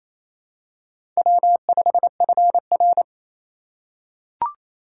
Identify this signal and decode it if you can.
Morse code, heard as a keyed tone, about 29 wpm, 700 Hz. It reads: W5FR
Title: USA Call Signs - 4 Characters - 3x Character Spacing - 28wpm